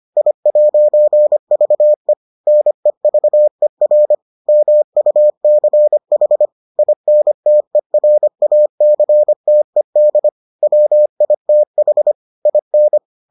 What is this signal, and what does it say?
Morse code, heard as a keyed tone, about 25 wpm, 600 Hz. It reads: I'VE NEVER MUCH INTERACTED WITH IN